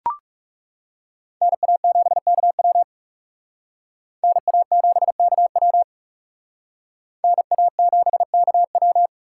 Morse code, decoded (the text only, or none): NA7KW NA7KW NA7KW